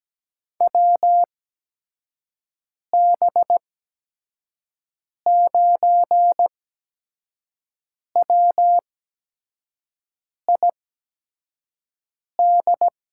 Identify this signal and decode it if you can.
Morse code, heard as a keyed tone, about 17 wpm, spaced as for 5 wpm, 700 Hz. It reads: WB9WID
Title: USA Call Signs - 6 Characters - 8x Character Spacing - 17wpm